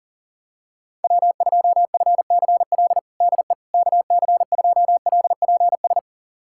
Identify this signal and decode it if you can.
Morse code, heard as a keyed tone, about 37 wpm, 700 Hz. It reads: W2FCL DE KC2LPS
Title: USA Callsign DE Callsign 40wpm